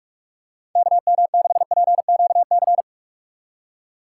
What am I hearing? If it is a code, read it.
Morse code, heard as a keyed tone, about 46 wpm, 700 Hz. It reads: KM6PQC
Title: USA Call Signs - 6 Characters - 4x Character Spacing - 45wpm